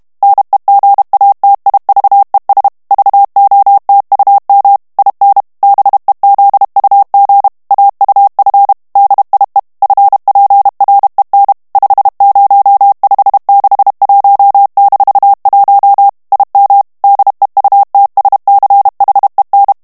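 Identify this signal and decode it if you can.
Morse code, heard as a keyed tone, about 32 wpm, 800 Hz. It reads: NEGATIVES VOTUM IN BEZUG AUF DIE FPREN 50561-1 IM DEUTSCHEN
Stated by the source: German plain-text Morse reading practice at 30 words per minute.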